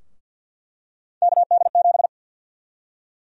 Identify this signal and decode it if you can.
Morse code, heard as a keyed tone, about 48 wpm, 700 Hz. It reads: KD7
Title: USA Call Sign Prefixes - Repeated 2x - 50wpm